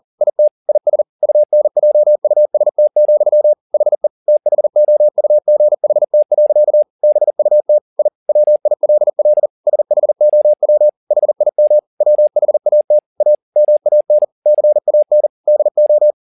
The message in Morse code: IT IS UNJUST, HE THOUGHT. BUT I WILL SHOW HIM WHAT A MAN CAN DO

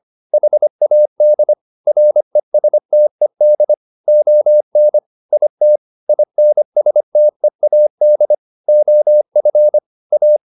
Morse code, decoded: HAD RESTED ON IT INSTEAD OF A

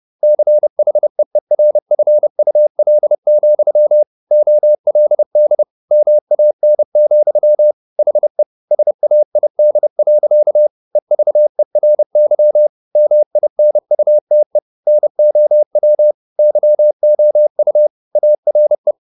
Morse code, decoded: CHEERFUL, OLD MAN, HE SAID. EVERY MINUTE NOW YOU ARE

